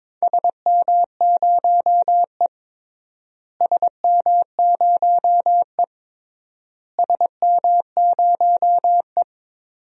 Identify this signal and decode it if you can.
Morse code, heard as a keyed tone, about 21 wpm, 700 Hz. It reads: SM0E SM0E SM0E
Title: DXCC-100 Call Signs - 4 Characters - Repeated 3x - 22wpm